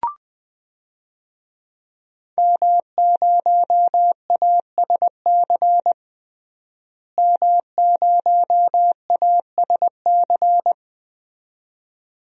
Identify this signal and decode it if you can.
Morse code, heard as a keyed tone, about 20 wpm, 700 Hz. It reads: M0ASC M0ASC
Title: DXCC-100 Call Signs - 5 Characters - Repeated 2x - 20wpm